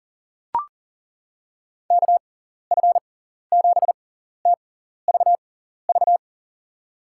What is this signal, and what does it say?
Morse code, heard as a keyed tone, about 40 wpm, spaced as for 14 wpm, 700 Hz. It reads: KF7TVV
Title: USA Call Signs - 6 Characters - 6x Character Spacing - 40wpm